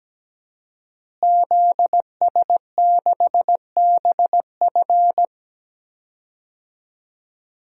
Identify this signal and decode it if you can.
Morse code, heard as a keyed tone, about 17 wpm, 700 Hz. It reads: ZS6BF
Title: DXCC-100 Call Signs - 5 Characters - Repeated 2x - 17wpm